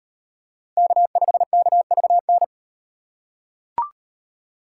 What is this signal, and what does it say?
Morse code, heard as a keyed tone, about 38 wpm, 700 Hz. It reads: K5KVN